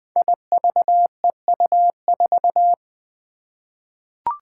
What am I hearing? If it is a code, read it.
Morse code, heard as a keyed tone, about 20 wpm, 700 Hz. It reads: IVEU4
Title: Five Challenging Letter-Number Combinations 20wpm